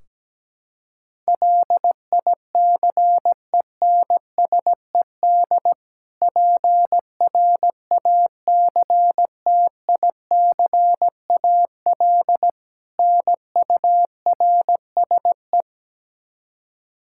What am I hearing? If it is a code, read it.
Morse code, heard as a keyed tone, about 17 wpm, 700 Hz. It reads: LICENSED PRACTICAL NURSE
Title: Practice QSO - Exchange 3 - More Info - 17wpm